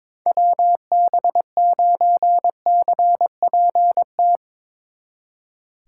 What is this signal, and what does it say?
Morse code, heard as a keyed tone, about 22 wpm, 700 Hz. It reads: WB9CPT
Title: Speed Racing - USA 2x3 Call Signs 22wpm